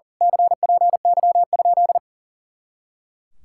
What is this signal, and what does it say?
Morse code, heard as a keyed tone, about 40 wpm, 700 Hz. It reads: CPY?
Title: Speed-Racing Common QSO Phrases - 40wpm